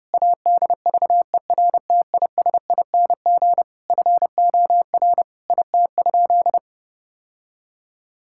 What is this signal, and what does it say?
Morse code, heard as a keyed tone, about 28 wpm, 700 Hz. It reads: ADVERTISING FOR IT?
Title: Sentences from Top 1000 Words - Review Part 6 - 30wpm